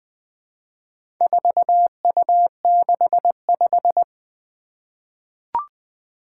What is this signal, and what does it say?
Morse code, heard as a keyed tone, about 20 wpm, 700 Hz. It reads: 4U65
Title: Four Challenging Letter-Number Combinations 20wpm